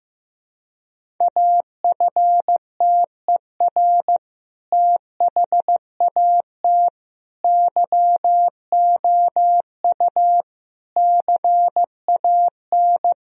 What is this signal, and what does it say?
Morse code, heard as a keyed tone, about 15 wpm, 700 Hz. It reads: AFTER THAT YOU CAN
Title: Sentences from Top 1000 Words - Review Part 2 - 15wpm